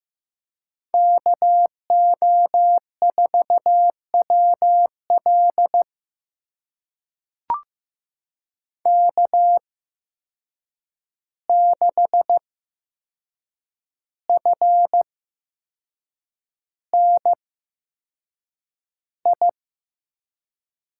Morse code, heard as a keyed tone, about 15 wpm, spaced as for 6 wpm, 700 Hz. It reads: KO4WL K6FNI
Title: USA Call Signs - 5 Characters - 8x Character Spacing - 15wpm